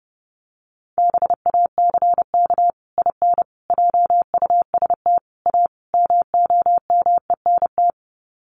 Morse code, heard as a keyed tone, about 28 wpm, 700 Hz. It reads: BACK IN JUST A MOMENT
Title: Sets of 5 Words Encore 30wpm